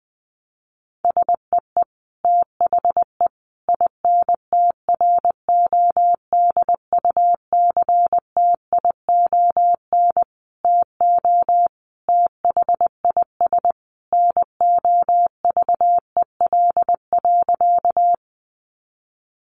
Morse code, heard as a keyed tone, about 20 wpm, 700 Hz. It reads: SEE THE INTRODUCTION TO THIS NOVEL.
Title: Sentences from Top 1300 Words 20wpm